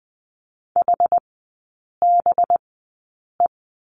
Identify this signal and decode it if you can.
Morse code, heard as a keyed tone, about 20 wpm, spaced as for 8 wpm, 700 Hz. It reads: HBE
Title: 184 Lesson 30e - Letter b - Three Characters 20wpm